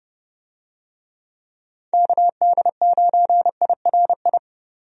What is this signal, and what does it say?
Morse code, heard as a keyed tone, about 30 wpm, 700 Hz. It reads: KD9IRI
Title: USA Call Signs - 6 Characters - 30wpm